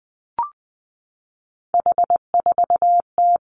Morse code, heard as a keyed tone, about 19 wpm, 700 Hz. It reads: H4T
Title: Three Challenging Letter-Number Combinations 20wpm